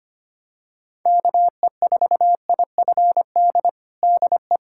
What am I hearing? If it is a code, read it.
Morse code, heard as a keyed tone, about 25 wpm, 700 Hz. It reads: KE4IFD DE
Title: USA Callsign DE Callsign - 6 Characters - 25wpm